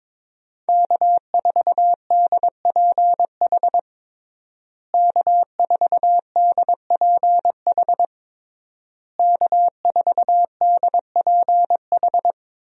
Morse code, heard as a keyed tone, about 22 wpm, 700 Hz. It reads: K4DPH K4DPH K4DPH
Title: USA Call Signs - 5 Characters - Repeated 3x - 22wpm